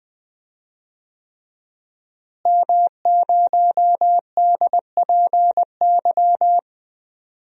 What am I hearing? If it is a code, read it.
Morse code, heard as a keyed tone, about 20 wpm, 700 Hz. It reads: M0DPY
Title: DXCC-100 Call Signs - 5 Characters - Repeated 3x - 20wpm